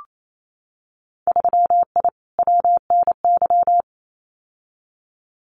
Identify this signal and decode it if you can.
Morse code, heard as a keyed tone, about 28 wpm, 700 Hz. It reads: 3I WNY
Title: Winter Field Day Exchanges Encore 28wpm